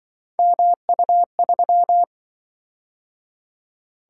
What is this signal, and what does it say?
Morse code, heard as a keyed tone, about 26 wpm, 700 Hz. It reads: MU3